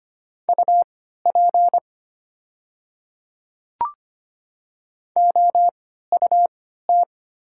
Morse code, heard as a keyed tone, about 25 wpm, spaced as for 14 wpm, 700 Hz. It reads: UP OUT
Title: 3x Character Spacing - Top 500 Words - 25wpm